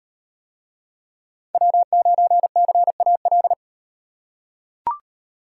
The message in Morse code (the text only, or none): W9CAL